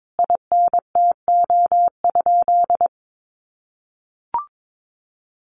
Morse code, heard as a keyed tone, about 22 wpm, 700 Hz. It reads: INTO?
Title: Sentences from Top 600 Words 22wpm